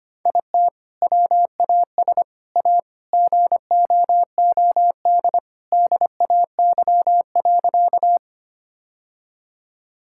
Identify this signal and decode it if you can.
Morse code, heard as a keyed tone, about 25 wpm, 700 Hz. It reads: IT WAS A GOOD DAY.
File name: practice / 25wpm / Sentences with 2-Word Phrases - 25wpm